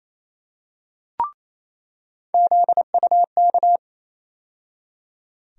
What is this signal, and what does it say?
Morse code, heard as a keyed tone, about 28 wpm, 700 Hz. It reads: ZUK
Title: USA Call Sign Postfixes - 3 Letters 28wpm